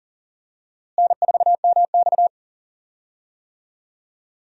Morse code, heard as a keyed tone, about 40 wpm, 700 Hz. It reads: N4MX
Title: USA Call Signs - 4 Characters - Repeated 3x - 40wpm